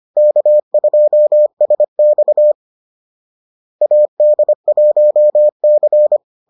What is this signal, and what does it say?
Morse code, heard as a keyed tone, about 25 wpm, 600 Hz. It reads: K2SX AD1C